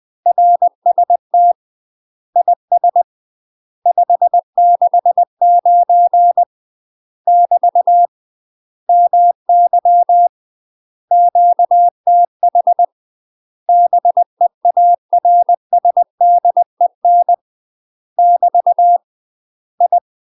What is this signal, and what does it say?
Morse code, heard as a keyed tone, about 20 wpm, 700 Hz. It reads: RST IS 569 <BT> MY QTH BEARSDEN <BT> I
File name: UK MASTER QSOs TRAINING/20WPM